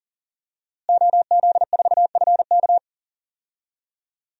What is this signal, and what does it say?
Morse code, heard as a keyed tone, about 40 wpm, 700 Hz. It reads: OZ4FK
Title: DXCC-100 Call Signs - 5 Characters - Repeated 2x - 40wpm